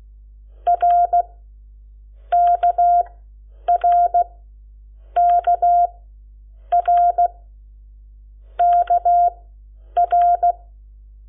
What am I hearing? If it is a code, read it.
Morse code, heard as a keyed tone, about 16 wpm, 700 Hz. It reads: R K R K R K R
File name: Lecciones / CLASE Nº7